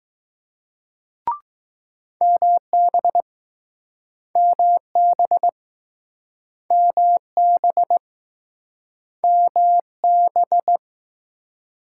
Morse code, heard as a keyed tone, about 21 wpm, 700 Hz. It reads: MB MB MB MB